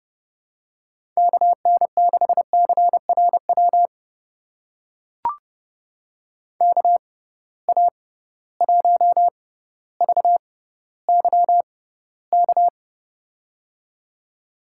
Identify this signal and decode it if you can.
Morse code, heard as a keyed tone, about 32 wpm, spaced as for 16 wpm, 700 Hz. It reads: KN6CRW KA1VYK
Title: USA Call Signs - 6 Characters - 6x Character Spacing - 30wpm